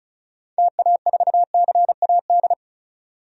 Morse code, heard as a keyed tone, about 36 wpm, 700 Hz. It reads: TA4CAD
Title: DXCC-100 Call Signs - 6 Characters - 35wpm